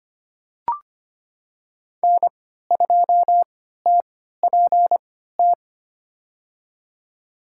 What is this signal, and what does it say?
Morse code, heard as a keyed tone, about 25 wpm, spaced as for 14 wpm, 700 Hz. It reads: N2TPT
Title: USA Call Signs - 5 Characters - 3x Character Spacing - 25wpm